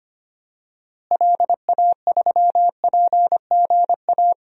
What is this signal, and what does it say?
Morse code, heard as a keyed tone, about 25 wpm, 700 Hz. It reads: LA3PGA
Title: DXCC-100 Call Signs - 6 Characters - Repeated 3x - 25wpm